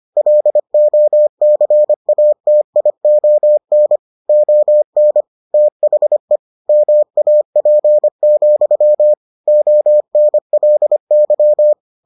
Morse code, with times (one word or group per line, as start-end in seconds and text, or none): 0.2-4.0 LOCATION
4.3-5.2 ON
5.5-6.4 THE
6.7-9.1 MAP,
9.5-11.7 ONLY